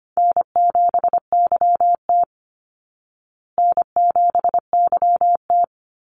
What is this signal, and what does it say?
Morse code, heard as a keyed tone, about 25 wpm, 700 Hz. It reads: N7YT N7YT
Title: Easy Call Signs Repeated 2x - 25wpm